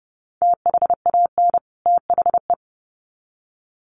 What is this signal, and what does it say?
Morse code, heard as a keyed tone, about 30 wpm, 700 Hz. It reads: THAN THE
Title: Sets of 2 Words 30wpm